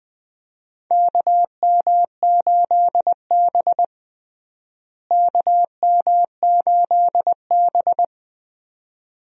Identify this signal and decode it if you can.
Morse code, heard as a keyed tone, about 21 wpm, 700 Hz. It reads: KM8B KM8B
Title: USA Call Signs - 4 Characters - Repeated 2x - 20wpm